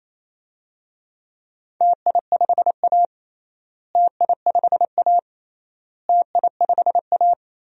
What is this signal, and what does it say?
Morse code, heard as a keyed tone, about 28 wpm, 700 Hz. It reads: TI5A TI5A TI5A
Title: DXCC-100 Call Signs - 4 Characters - Repeated 3x - 28wpm